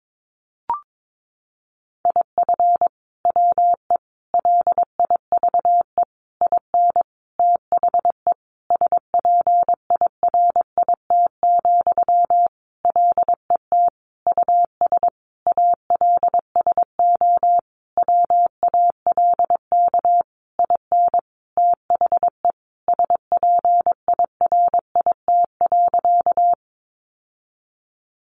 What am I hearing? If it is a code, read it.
Morse code, heard as a keyed tone, about 22 wpm, 700 Hz. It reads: IF WE LIVE IN THE SPIRIT, LET US ALSO WALK IN THE SPIRIT.
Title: Sentences from Top 1400 Words 22wpm